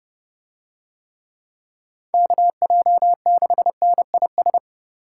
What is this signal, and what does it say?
Morse code, heard as a keyed tone, about 32 wpm, 700 Hz. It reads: KJ6NIS